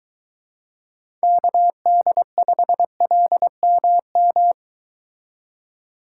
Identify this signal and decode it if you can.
Morse code, heard as a keyed tone, about 23 wpm, 700 Hz. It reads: KD5LMM